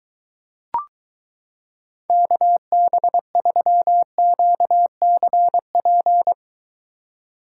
Morse code, heard as a keyed tone, about 23 wpm, 700 Hz. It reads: KB3QCP